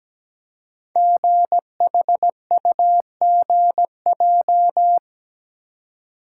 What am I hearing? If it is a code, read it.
Morse code, heard as a keyed tone, about 17 wpm, 700 Hz. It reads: GHUGJ
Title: Code Groups - 5 Letters - 17wpm